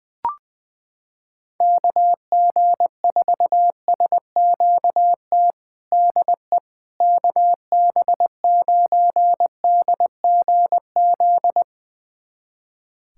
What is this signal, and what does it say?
Morse code, heard as a keyed tone, about 20 wpm, 700 Hz. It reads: KG4SQT DE KB9DGZ
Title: USA Callsign DE Callsign - 6 Characters - 20wpm